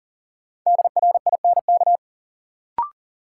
Morse code, heard as a keyed tone, about 40 wpm, 700 Hz. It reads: DRINK
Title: Sentences from Top 1500 Words - New Word Drill - 40wpm